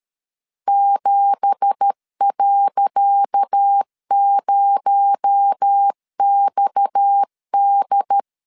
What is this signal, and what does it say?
Morse code, heard as a keyed tone, about 13 wpm, 800 Hz. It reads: 7.0XD
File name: LEZIONI / 10 SIGNS / 10C DOT 12wpm 2451